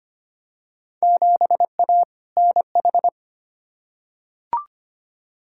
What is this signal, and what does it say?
Morse code, heard as a keyed tone, about 25 wpm, 700 Hz. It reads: 7A NH
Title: Field Day Exchanges - 25wpm